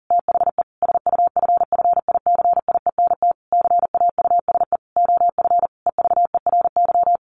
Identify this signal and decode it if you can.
Morse code, heard as a keyed tone, about 40 wpm, 700 Hz. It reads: THE SUFFICIENT CAUSE OF EVERY
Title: Sentences from Top 1400 Words 40wpm